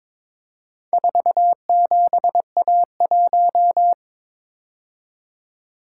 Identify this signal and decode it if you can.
Morse code, heard as a keyed tone, about 22 wpm, 700 Hz. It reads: 47A1
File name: practice / 22wpm / Four Letter-Number Combinations - 22wpm